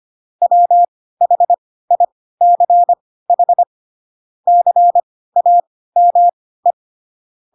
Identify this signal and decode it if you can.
Morse code, heard as a keyed tone, about 25 wpm, spaced as for 16 wpm, 700 Hz. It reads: WHICH CAME
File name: Pinocchio / Pinocchio ch2 10 25 ik1hge